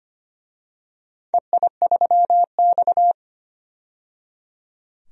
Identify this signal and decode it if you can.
Morse code, heard as a keyed tone, about 25 wpm, 700 Hz. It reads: EI3X
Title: DXCC-100 Call Signs - 4 Characters - 25wpm